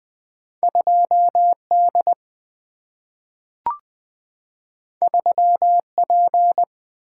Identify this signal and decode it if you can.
Morse code, heard as a keyed tone, about 20 wpm, 700 Hz. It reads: 2D 3P
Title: Single-Letter Single-Number 20wpm